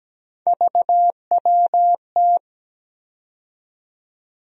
Morse code, heard as a keyed tone, about 17 wpm, 700 Hz. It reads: VWT
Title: USA Call Sign Postfixes - 3 Letters 17wpm